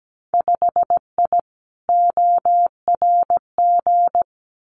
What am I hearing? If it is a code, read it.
Morse code, heard as a keyed tone, about 17 wpm, 700 Hz. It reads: 5I ORG
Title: Winter Field Day Exchanges Encore 17wpm